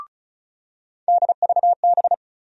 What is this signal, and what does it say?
Morse code, heard as a keyed tone, about 36 wpm, 700 Hz. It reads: DVB
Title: USA Call Sign Postfixes - 3 Letters 35wpm